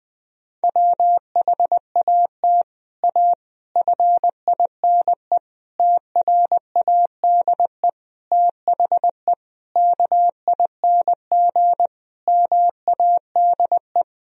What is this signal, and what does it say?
Morse code, heard as a keyed tone, about 20 wpm, 700 Hz. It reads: WHAT A FINE TRADE THE KING MADE
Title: Sentences from Top 1000 Words - Review Part 4 - 20wpm